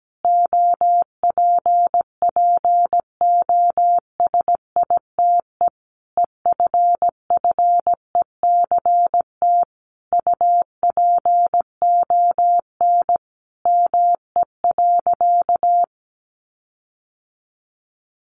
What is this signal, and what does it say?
Morse code, heard as a keyed tone, about 17 wpm, 700 Hz. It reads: OPPOSITE EFFECT UPON ME.
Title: Sentences from Top 1300 Words 17wpm